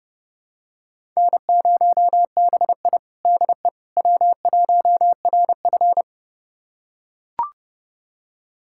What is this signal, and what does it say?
Morse code, heard as a keyed tone, about 30 wpm, 700 Hz. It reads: N0BI DE W1RF
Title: USA Callsign DE Callsign - 4 Characters - 30wpm